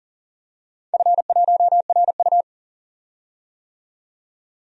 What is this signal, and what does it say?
Morse code, heard as a keyed tone, about 41 wpm, 700 Hz. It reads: F1RU
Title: DXCC-100 Call Signs - 4 Characters - 40wpm